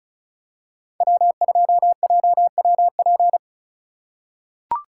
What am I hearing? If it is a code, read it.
Morse code, heard as a keyed tone, about 36 wpm, 700 Hz. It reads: W2JWP
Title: USA Call Signs - 4x Character Spacing - 35wpm